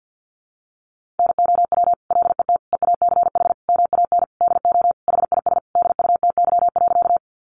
Morse code, heard as a keyed tone, about 50 wpm, 700 Hz. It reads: NOW LET EACH MAN DO HIS DUTY.
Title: Sentences from Top 1500 Words 50wpm